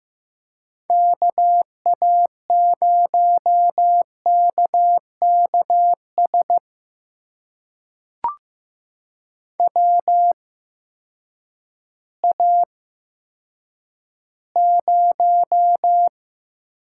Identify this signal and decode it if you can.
Morse code, heard as a keyed tone, about 15 wpm, spaced as for 9 wpm, 700 Hz. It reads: KA0KKS WA0
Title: USA Call Signs - 6 Characters - 8x Character Spacing - 15wpm